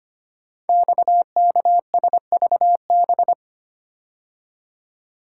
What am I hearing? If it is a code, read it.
Morse code, heard as a keyed tone, about 25 wpm, 700 Hz. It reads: XKSVB
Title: Code Groups - 5 Letters - 25wpm